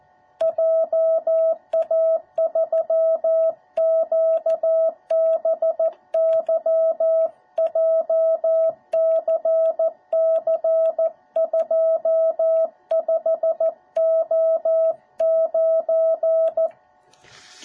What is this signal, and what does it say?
Morse code, heard as a keyed tone, about 14 wpm, 650 Hz. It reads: JA3QBY JCC25O9